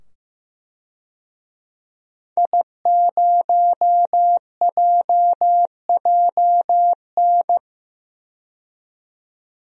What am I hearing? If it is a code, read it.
Morse code, heard as a keyed tone, about 15 wpm, 700 Hz. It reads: I0JJN
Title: DXCC-100 Call Signs - 5 Characters - 15wpm